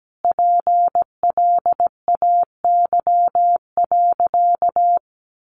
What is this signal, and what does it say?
Morse code, heard as a keyed tone, about 17 wpm, 700 Hz. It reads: PLAY.
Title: Sentences from Top 1500 Words 17wpm